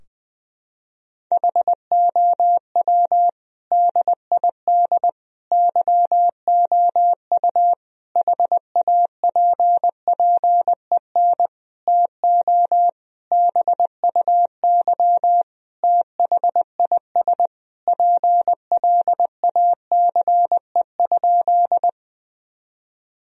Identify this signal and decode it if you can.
Morse code, heard as a keyed tone, about 20 wpm, 700 Hz. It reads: HOW DID YOU HAPPEN TO BUY THIS PLACE?
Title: Sentences from Top 1000 Words - Review Part 4 - 20wpm